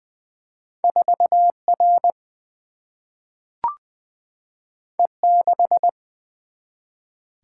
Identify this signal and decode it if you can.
Morse code, heard as a keyed tone, about 20 wpm, 700 Hz. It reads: 4R E6
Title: Single-Letter Single-Number 20wpm